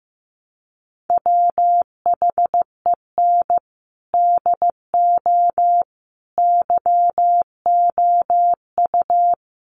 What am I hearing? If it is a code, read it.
Morse code, heard as a keyed tone, about 15 wpm, 700 Hz. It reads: WHEN DO YOU
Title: Sentences from Top 1100 Words 15wpm